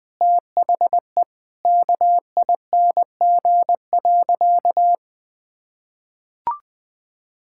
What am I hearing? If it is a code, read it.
Morse code, heard as a keyed tone, about 20 wpm, 700 Hz. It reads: THE KING.
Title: Sentences from Top 1000 Words - Review Part 6 - 20wpm